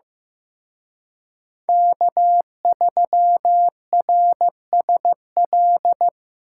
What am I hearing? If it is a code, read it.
Morse code, heard as a keyed tone, about 18 wpm, 700 Hz. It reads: K3RSL
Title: Speed Racing - USA 1x3 Call Signs 15wpm